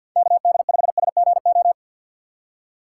Morse code, heard as a keyed tone, about 51 wpm, 700 Hz. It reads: KD5SGO
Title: USA Call Signs - 6 Characters - 6x Character Spacing - 50wpm